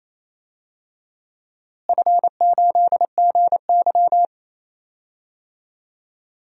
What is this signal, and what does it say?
Morse code, heard as a keyed tone, about 28 wpm, 700 Hz. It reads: F8GY
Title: DXCC-100 Call Signs - 4 Characters - Repeated 2x - 28wpm